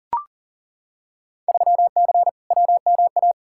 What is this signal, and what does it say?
Morse code, heard as a keyed tone, about 40 wpm, 700 Hz. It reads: 3C WMA
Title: Field Day Exchanges Encore - 40wpm